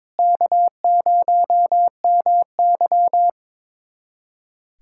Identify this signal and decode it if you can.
Morse code, heard as a keyed tone, about 21 wpm, 700 Hz. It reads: K0MY
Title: USA Callsign DE Callsign - 4 Characters - 22wpm